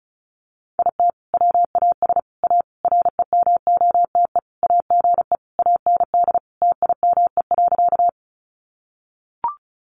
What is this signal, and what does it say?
Morse code, heard as a keyed tone, about 34 wpm, 700 Hz. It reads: IT WAS A REMOTE AGE AND TIME.
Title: Sentences from Top 1300 Words 35wpm